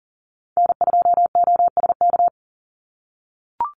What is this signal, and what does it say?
Morse code, heard as a keyed tone, about 40 wpm, 700 Hz. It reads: N2OSK
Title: USA Call Signs 40wpm